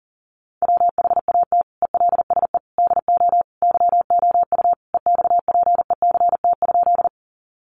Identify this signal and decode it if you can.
Morse code, heard as a keyed tone, about 40 wpm, 700 Hz. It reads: WHAT ELSE DO YOU EXPECT?
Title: Sentences from Top 1100 Words 40wpm